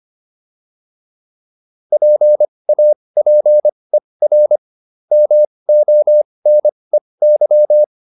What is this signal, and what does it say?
Morse code, heard as a keyed tone, about 25 wpm, spaced as for 20 wpm, 600 Hz. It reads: PAPER MONEY